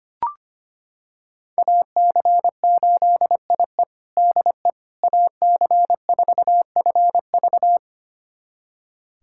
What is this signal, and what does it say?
Morse code, heard as a keyed tone, about 25 wpm, 700 Hz. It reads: AC8IE DE AC4FV
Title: USA Callsign DE Callsign - 5 Characters - 25wpm